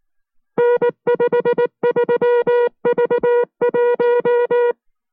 Morse code, heard as a keyed tone, about 19 wpm, 480 Hz. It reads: N53V1